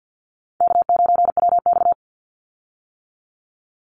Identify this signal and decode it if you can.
Morse code, heard as a keyed tone, about 50 wpm, 700 Hz. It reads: K9WX
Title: Easy Call Signs - 50wpm